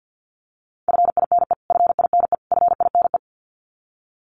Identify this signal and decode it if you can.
Morse code, heard as a keyed tone, about 50 wpm, 700 Hz. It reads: FINE FINE FINE
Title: Top 500 Words Repeated 50wpm